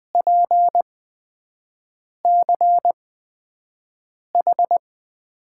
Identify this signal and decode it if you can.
Morse code, heard as a keyed tone, about 20 wpm, spaced as for 5 wpm, 700 Hz. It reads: PCH
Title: USA Call Signs - 6 Characters - 8x Character Spacing - 20wpm